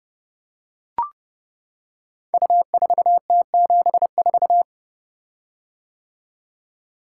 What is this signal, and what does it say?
Morse code, heard as a keyed tone, about 31 wpm, 700 Hz. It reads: U4T74
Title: Code Groups - 5 Letter:Numbers - 30wpm